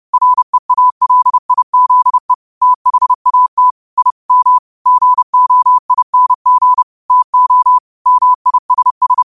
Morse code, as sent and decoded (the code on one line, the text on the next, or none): .-. .- .-.. .. --.. / - .... .- - / .. / -- / --. --- .. -. --. / - --- / -- .. ... ...
REALIZE THAT I M GOING TO MISS